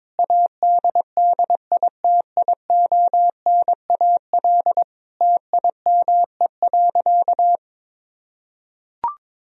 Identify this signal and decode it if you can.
Morse code, heard as a keyed tone, about 22 wpm, 700 Hz. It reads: ADDITIONAL TIME.
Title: Sentences from Top 1000 Words - Review Part 4 - 22wpm